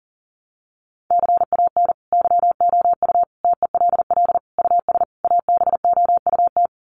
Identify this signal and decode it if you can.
Morse code, heard as a keyed tone, about 40 wpm, 700 Hz. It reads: CAN YOU TELL US ABOUT
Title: Sets of 5 Words Encore2 40wpm